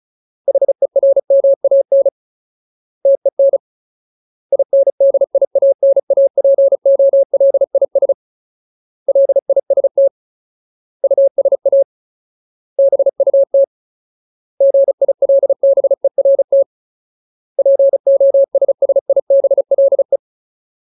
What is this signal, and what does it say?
Morse code, heard as a keyed tone, about 34 wpm, 550 Hz. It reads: HERMAN TEN INDIANAPOLIS LIST USA BUT GILBERT POSSIBLE
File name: Ultimate Word List / 35WPM Ultimate